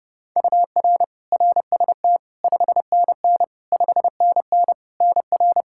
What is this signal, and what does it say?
Morse code, heard as a keyed tone, about 30 wpm, 700 Hz. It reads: UR RST 5NN 5NN NR
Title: Practice QSO - Exchange 1 - Beginning - 30wpm